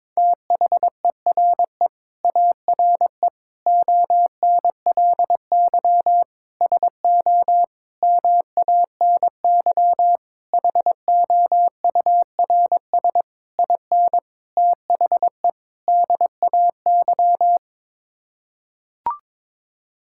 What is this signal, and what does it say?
Morse code, heard as a keyed tone, about 21 wpm, 700 Hz. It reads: THERE ARE ONLY SO MANY HOURS IN THE DAY
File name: Sets of 9 Words - 22wpm